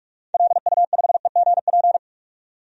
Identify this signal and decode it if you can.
Morse code, heard as a keyed tone, about 41 wpm, 700 Hz. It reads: LU5EGP
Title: DXCC-100 Call Signs - 6 Characters - 45wpm